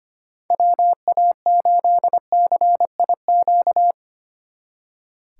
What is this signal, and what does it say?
Morse code, heard as a keyed tone, about 33 wpm, 700 Hz. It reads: WA8CIQ